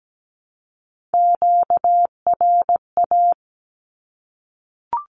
Morse code, heard as a keyed tone, about 17 wpm, 700 Hz. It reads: QRA
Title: Q-Codes 17wpm